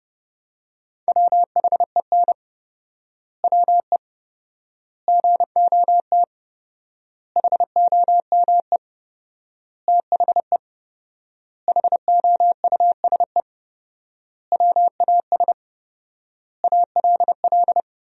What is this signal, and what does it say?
Morse code, heard as a keyed tone, about 30 wpm, 700 Hz. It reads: WHEN WE GOT HOME THE HOUSE WAS ALL
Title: 4x Word-Spacing - Sentences from Top 500 Words - 30wpm